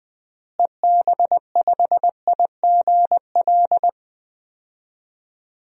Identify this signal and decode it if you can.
Morse code, heard as a keyed tone, about 21 wpm, 700 Hz. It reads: EB5IGL
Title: DXCC-100 Call Signs - 6 Characters - Repeated 3x - 20wpm